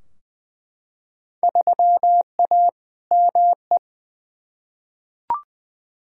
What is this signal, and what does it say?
Morse code, heard as a keyed tone, about 20 wpm, 700 Hz. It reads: 3A ME
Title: Field Day Exchanges - 20wpm